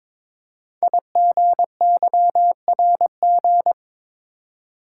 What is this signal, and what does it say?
Morse code, heard as a keyed tone, about 22 wpm, 700 Hz. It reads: IGYRG
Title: Code Groups - 5 Letters - 22wpm